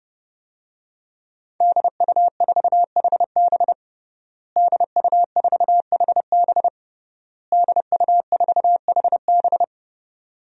Morse code, heard as a keyed tone, about 30 wpm, 700 Hz. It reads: DU4HB DU4HB DU4HB
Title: DXCC-100 Call Signs - 5 Characters - Repeated 3x - 30wpm